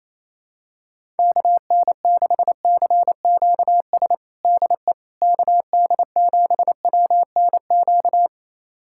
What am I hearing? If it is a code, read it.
Morse code, heard as a keyed tone, about 27 wpm, 700 Hz. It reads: KN6CQS DE KD7WNQ
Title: USA Callsign DE Callsign - 6 Characters - 28wpm